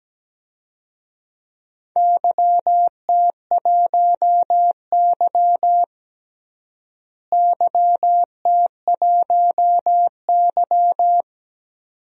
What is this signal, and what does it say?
Morse code, heard as a keyed tone, about 17 wpm, 700 Hz. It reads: YT1Y YT1Y
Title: DXCC-100 Call Signs - 4 Characters - Repeated 2x - 17wpm